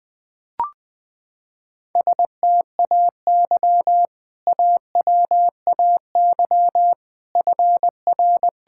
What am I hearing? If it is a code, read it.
Morse code, heard as a keyed tone, about 21 wpm, 700 Hz. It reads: STAY AWAY FR
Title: Sentences with 3-Word Phrases - 20wpm